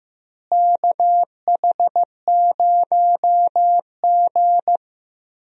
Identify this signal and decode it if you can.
Morse code, heard as a keyed tone, about 15 wpm, 700 Hz. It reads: KH0G
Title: USA Call Signs - 4 Characters - Repeated 3x - 15wpm